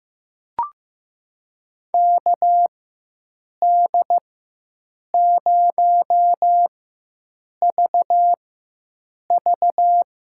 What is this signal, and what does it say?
Morse code, heard as a keyed tone, about 15 wpm, spaced as for 7 wpm, 700 Hz. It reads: KD0VV
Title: USA Call Signs - 4x Character Spacing - 15wpm